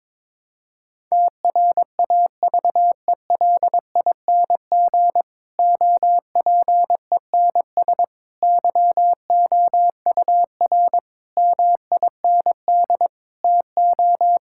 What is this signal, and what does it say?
Morse code, heard as a keyed tone, about 22 wpm, 700 Hz. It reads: TRAVELING OPENS YOUR MIND TO
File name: Difficult Sentences with Hints - 22wpm